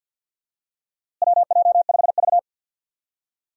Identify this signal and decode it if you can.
Morse code, heard as a keyed tone, about 47 wpm, 700 Hz. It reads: WJ5V
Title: USA Call Signs - 4 Characters - Repeated 2x - 50wpm